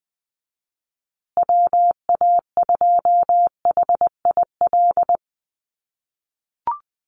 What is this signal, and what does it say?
Morse code, heard as a keyed tone, about 20 wpm, 700 Hz. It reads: WA2HIL
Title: USA Call Signs Repeated 3x 20wpm